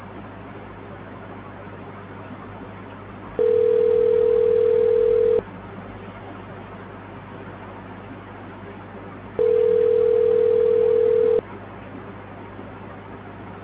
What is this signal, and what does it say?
Ringback tone (North America), 440 + 480 Hz.